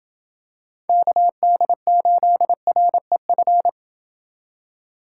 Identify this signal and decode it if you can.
Morse code, heard as a keyed tone, about 27 wpm, 700 Hz. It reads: KD8REF